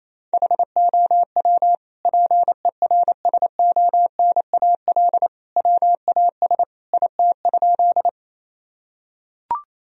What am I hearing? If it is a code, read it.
Morse code, heard as a keyed tone, about 27 wpm, 700 Hz. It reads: HOW PERSONAL WAS IT?
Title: Sentences from Top 1000 Words - Review Part 4 - 28wpm